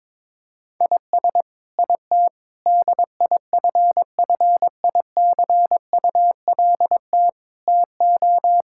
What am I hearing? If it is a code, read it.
Morse code, heard as a keyed tone, about 22 wpm, 700 Hz. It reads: IS IT DIFFICULT TO
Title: Sentences from Top 1000 Words - Review Part 5 - 22wpm